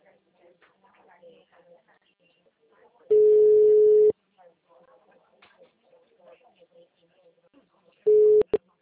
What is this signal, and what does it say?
Ringback tone (Europe, 425 Hz), with speech.